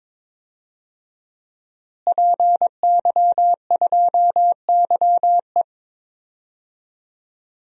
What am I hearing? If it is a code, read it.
Morse code, heard as a keyed tone, about 22 wpm, 700 Hz. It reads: PY2YE
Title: DXCC-100 Call Signs - 5 Characters - Repeated 2x - 22wpm